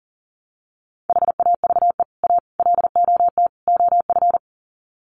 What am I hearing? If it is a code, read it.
Morse code, heard as a keyed tone, about 40 wpm, 700 Hz. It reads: HAVE A LOT OF
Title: Sets of 4 Words 40wpm